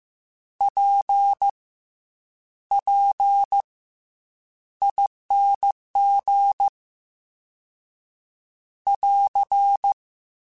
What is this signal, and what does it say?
Morse code, heard as a keyed tone, about 15 wpm, spaced as for 8 wpm, 800 Hz. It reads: PPING <AR>